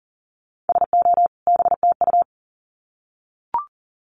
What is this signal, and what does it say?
Morse code, heard as a keyed tone, about 40 wpm, 700 Hz. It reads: SO BTU
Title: Common QSO Phrases 40wpm